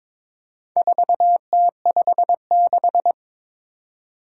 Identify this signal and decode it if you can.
Morse code, heard as a keyed tone, about 22 wpm, 700 Hz. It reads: 4T56
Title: Four Challenging Letter-Number Combinations 22wpm